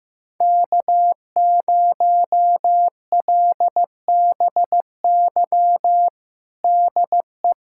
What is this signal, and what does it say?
Morse code, heard as a keyed tone, about 15 wpm, 700 Hz. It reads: K0LBY DE
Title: USA Callsign DE Callsign - 5 Characters - 15wpm